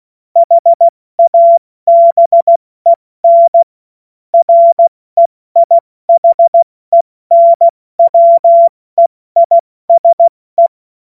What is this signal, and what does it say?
Morse code, heard as a keyed tone, about 16 wpm, 680 Hz. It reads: HABEN REIHENWEISE
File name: Morsekurs / Witze / Witz 19. Oktober 2021, 70 F80